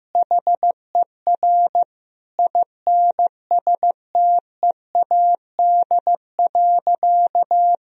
Morse code, heard as a keyed tone, about 15 wpm, 700 Hz. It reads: HER INSTEAD.
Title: Sentences from Top 1000 Words - Review Part 4 - 15wpm